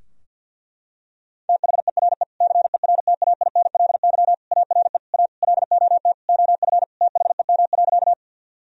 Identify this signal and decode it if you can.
Morse code, heard as a keyed tone, about 46 wpm, 700 Hz. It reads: THERE CERTAINLY ARE A LOT OF THEM.
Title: Sentences from Top 1000 Words - Review Part 3 - 50wpm